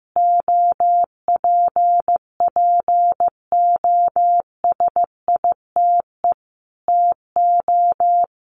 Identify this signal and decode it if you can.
Morse code, heard as a keyed tone, about 15 wpm, 700 Hz. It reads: OPPOSITE TO
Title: Sentences from Top 1300 Words 15wpm